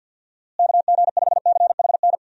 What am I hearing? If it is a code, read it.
Morse code, heard as a keyed tone, about 52 wpm, 700 Hz. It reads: KG5CHN
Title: USA Call Signs - 6 Characters - Repeated 3x - 50wpm